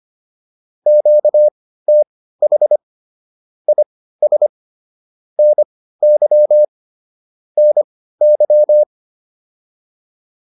Morse code, heard as a keyed tone, about 25 wpm, spaced as for 15 wpm, 600 Hz. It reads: QTH IS NY NY